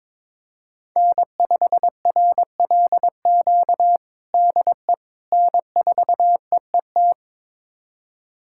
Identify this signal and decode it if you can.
Morse code, heard as a keyed tone, about 22 wpm, 700 Hz. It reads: N5RLQ DE N4EET
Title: USA Callsign DE Callsign - 5 Characters - 22wpm